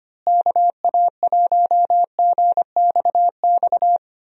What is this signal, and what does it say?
Morse code, heard as a keyed tone, about 25 wpm, 700 Hz. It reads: KA1GXX
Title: USA Call Signs - 6 Characters - Repeated 2x - 25wpm